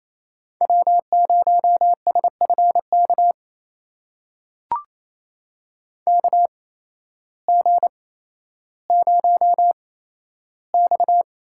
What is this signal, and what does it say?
Morse code, heard as a keyed tone, about 28 wpm, spaced as for 13 wpm, 700 Hz. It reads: W0SFK KG0X
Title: USA Call Signs - 5 Characters - 8x Character Spacing - 28wpm